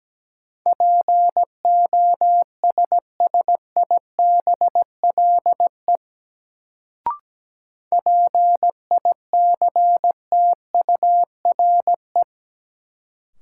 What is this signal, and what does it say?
Morse code, heard as a keyed tone, about 17 wpm, 700 Hz. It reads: POSSIBLE PICTURE
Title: Sentences from Top 500 Words - New Word Drill - 17wpm